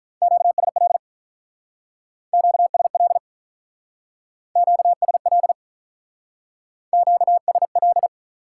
Morse code, heard as a keyed tone, about 44 wpm, 700 Hz. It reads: QSL QSL QSL QSL